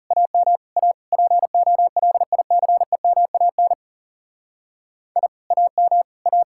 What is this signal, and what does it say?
Morse code, heard as a keyed tone, about 35 wpm, 700 Hz. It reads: AM A POLICEMAN I AM A
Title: Speed-Racing Common QSO Phrases - 30wpm